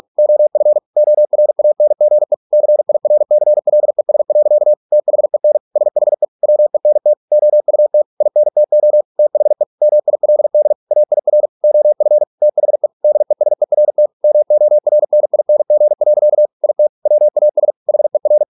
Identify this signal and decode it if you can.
Morse code, heard as a keyed tone, about 46 wpm, 600 Hz. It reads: OF ORANGE CIRCLES. THEN SHE WENT OUT INTO THE MILD AIR OF THE DESERT MORNING. IT WAS HER